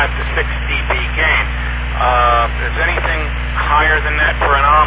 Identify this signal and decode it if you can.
Emergency Alert System (SAME) tones, heard as harsh buzzing bursts, with speech. It is heard from 1.1 to 4.3 s.